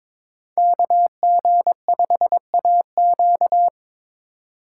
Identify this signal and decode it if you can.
Morse code, heard as a keyed tone, about 26 wpm, 700 Hz. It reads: KG5AQ
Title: Speed Racing - USA 2x2 Call Signs 22wpm